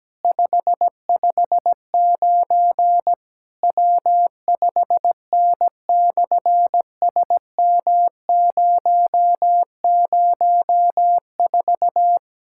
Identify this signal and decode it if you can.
Morse code, heard as a keyed tone, about 17 wpm, 700 Hz. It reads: 559 W5N/SM004
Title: SOTA QSO Practice 17wpm